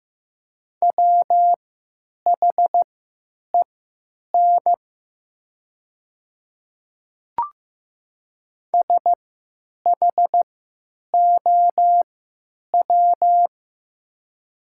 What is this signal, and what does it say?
Morse code, heard as a keyed tone, about 15 wpm, spaced as for 9 wpm, 700 Hz. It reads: WHEN SHOW
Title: 3x Character Spacing - Top 500 Words - 15wpm